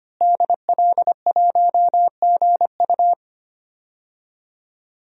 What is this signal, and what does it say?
Morse code, heard as a keyed tone, about 25 wpm, 700 Hz. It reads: DL1GU
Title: DXCC-100 Call Signs - 5 Characters - Repeated 2x - 25wpm